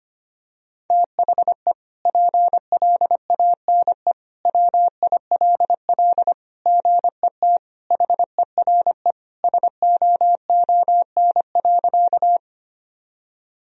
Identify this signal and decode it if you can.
Morse code, heard as a keyed tone, about 25 wpm, 700 Hz. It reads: THE PLANE WILL GET HERE SOON.
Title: Sentences from Top 1000 Words - Review Part 3 - 25wpm